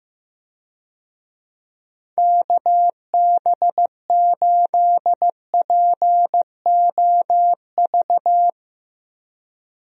Morse code, heard as a keyed tone, about 15 wpm, 700 Hz. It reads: KB8POV
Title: USA Call Signs - 6 Characters - 15wpm